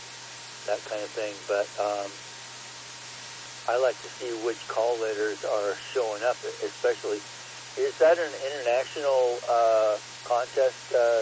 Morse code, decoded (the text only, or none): none